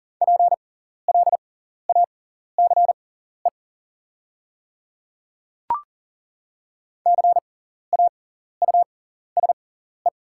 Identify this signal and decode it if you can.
Morse code, heard as a keyed tone, about 40 wpm, spaced as for 14 wpm, 700 Hz. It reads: PLACE CAUSE
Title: 6x Character Spacing - Top 300 Words - 40wpm